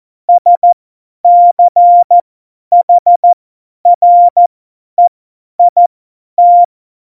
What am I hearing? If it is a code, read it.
Morse code, heard as a keyed tone, about 14 wpm, spaced as for 10 wpm, 700 Hz. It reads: SCHREIT